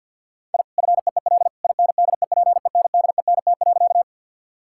Morse code, heard as a keyed tone, about 46 wpm, 700 Hz. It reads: I FEEL INDEPENDENT.
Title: Sentences from Top 1000 Words - Review Part 6 - 50wpm